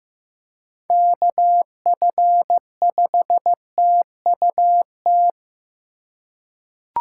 Morse code, heard as a keyed tone, about 15 wpm, 700 Hz. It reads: KF5TUT
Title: USA Call Signs - 6 Characters - 3x Character Spacing - 15wpm